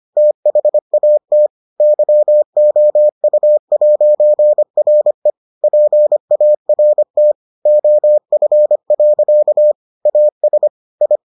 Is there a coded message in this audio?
Yes: Morse code from 0.2 to 11.2 s, heard as a keyed tone, about 25 wpm, 600 Hz. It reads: THAT YOU'RE PART OF. AS I